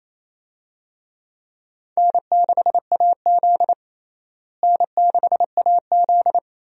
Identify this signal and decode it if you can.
Morse code, heard as a keyed tone, about 28 wpm, 700 Hz. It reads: N6AZ N6AZ
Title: USA Call Signs - 4 Characters - Repeated 2x - 28wpm